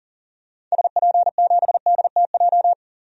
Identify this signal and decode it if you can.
Morse code, heard as a keyed tone, about 40 wpm, 700 Hz. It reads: SP7DTJ
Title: DXCC-100 Call Signs - 6 Characters - 40wpm